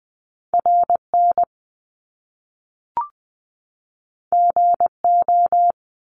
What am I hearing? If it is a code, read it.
Morse code, heard as a keyed tone, about 20 wpm, 700 Hz. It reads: R N G O
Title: Top Two Letter Combinations in Words 20wpm